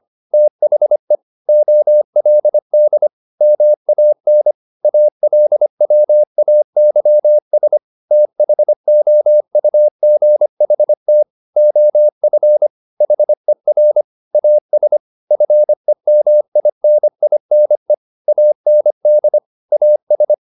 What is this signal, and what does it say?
Morse code, heard as a keyed tone, about 25 wpm, 600 Hz. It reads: THE OLD MAN ALWAYS THOUGHT OF HER AS FEMININE AND AS
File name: the old man and the sea part 1 25wpm